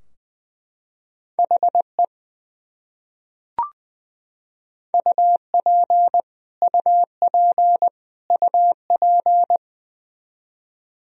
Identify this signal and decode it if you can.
Morse code, heard as a keyed tone, about 21 wpm, 700 Hz. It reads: HE UP UP UP
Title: Top 100 Words Repeated 20wpm